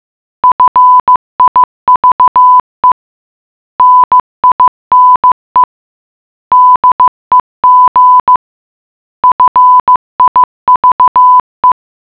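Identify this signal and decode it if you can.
Morse code, heard as a keyed tone, about 15 wpm, 1000 Hz. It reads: FIVE NINE DEG FIVE
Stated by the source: Morse puzzle for the geocache Lambertsetersenderen - Gacrux#8 (GCRHX0) in Oslo, Norway.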